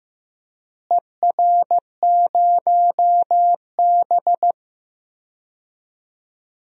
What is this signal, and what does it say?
Morse code, heard as a keyed tone, about 15 wpm, 700 Hz. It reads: ER0B
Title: DXCC-100 Call Signs - 4 Characters - Repeated 2x - 15wpm